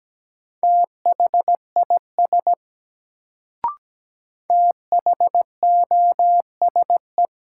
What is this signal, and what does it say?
Morse code, heard as a keyed tone, about 17 wpm, 700 Hz. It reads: THIS THOSE
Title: Sets of 2 Words - New Word Drill - 17wpm